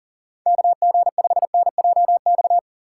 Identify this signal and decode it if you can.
Morse code, heard as a keyed tone, about 49 wpm, 700 Hz. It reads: KG5NJX
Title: Speed Racing - USA 2x3 Call Signs 40wpm